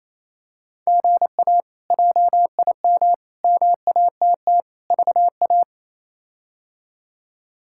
Morse code, heard as a keyed tone, about 27 wpm, 700 Hz. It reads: GA JIM MATT VA
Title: K1USN SST QSO Practice - 28wpm